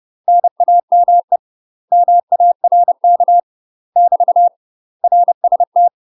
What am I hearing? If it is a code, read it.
Morse code, heard as a keyed tone, about 30 wpm, 700 Hz. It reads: NAME MARK <BT> RST
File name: UK MASTER QSOs TRAINING / 30WPM / UK MASTER QSOs 0008